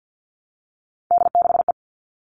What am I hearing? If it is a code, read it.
Morse code, heard as a keyed tone, about 50 wpm, 700 Hz. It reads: D6E
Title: All Cut Numbers - 3-Digit 50wpm